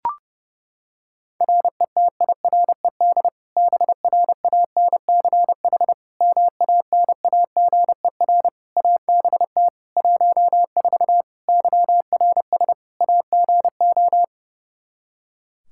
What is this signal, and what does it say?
Morse code, heard as a keyed tone, about 32 wpm, 700 Hz. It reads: RETIRED BRANCH MANAGER ABT 14 YRS AGO
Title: Practice QSO - Exchange 3 - More Info - 30wpm